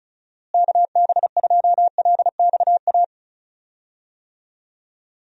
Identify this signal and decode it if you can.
Morse code, heard as a keyed tone, about 36 wpm, 700 Hz. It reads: KB2LXA
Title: USA Call Signs - 6 Characters - Repeated 3x - 35wpm